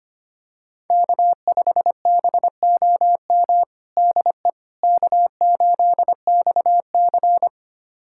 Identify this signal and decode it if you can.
Morse code, heard as a keyed tone, about 25 wpm, 700 Hz. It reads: K5BOM DE K8XC
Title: USA Callsign DE Callsign 25wpm